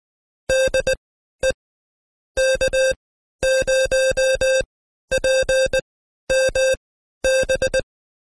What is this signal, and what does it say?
Morse code, heard as a keyed tone, about 20 wpm, spaced as for 12 wpm, 530 Hz. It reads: DE K0PMB
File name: Code Quick / AudioDownld2 / Seg132-Stepper5 11WPM